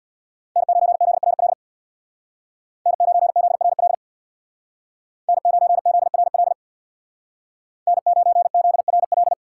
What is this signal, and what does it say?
Morse code, heard as a keyed tone, about 61 wpm, 700 Hz. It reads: N9ZRL N9ZRL N9ZRL N9ZRL
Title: Speed Racing - USA 1x3 Call Signs 50wpm